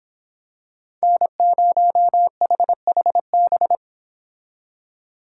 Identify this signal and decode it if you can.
Morse code, heard as a keyed tone, about 25 wpm, 700 Hz. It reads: N0HHB